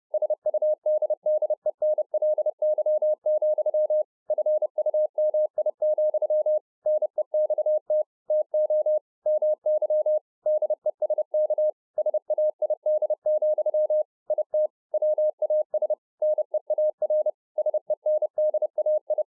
Morse code, heard as a keyed tone, about 30 wpm, 610 Hz. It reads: SUDDENLY, FUMI, NEXT TO MY DESK SAID, IT WAS NEAR SENDAI